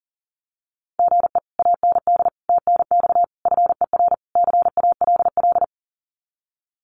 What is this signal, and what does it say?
Morse code, heard as a keyed tone, about 40 wpm, 700 Hz. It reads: GE AND TNX FER CALL
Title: Common QSO Phrases Encore 40wpm